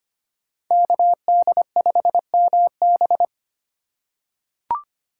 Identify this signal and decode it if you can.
Morse code, heard as a keyed tone, about 25 wpm, 700 Hz. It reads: KD5MB
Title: USA Call Signs - 5 Characters - 3x Character Spacing - 25wpm